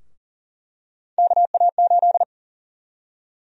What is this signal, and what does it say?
Morse code, heard as a keyed tone, about 37 wpm, 700 Hz. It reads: KA8
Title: USA Call Sign Prefixes - Repeated 2x - 40wpm